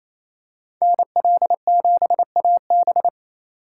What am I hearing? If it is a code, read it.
Morse code, heard as a keyed tone, about 28 wpm, 700 Hz. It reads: NL7AB
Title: USA Call Signs - 5 Characters - 2x Character Spacing - 28wpm